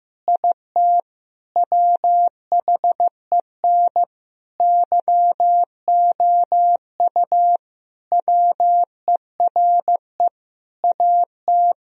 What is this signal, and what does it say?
Morse code, heard as a keyed tone, about 16 wpm, 700 Hz. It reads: IT WHEN YOU WERE AT
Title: Sentences from Top 1000 Words - Review Part 1 - 15wpm